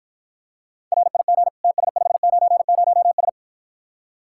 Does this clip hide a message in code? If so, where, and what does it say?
Yes: Morse code from 0.9 to 3.3 s, heard as a keyed tone, about 52 wpm, 700 Hz. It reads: RIG TS590S